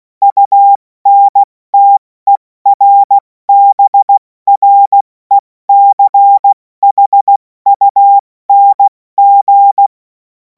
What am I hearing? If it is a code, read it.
Morse code, heard as a keyed tone, about 16 wpm, 800 Hz. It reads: UNTERBRECHUNG
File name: Morsekurs / Witze / Witz 28. April 2026 70 F80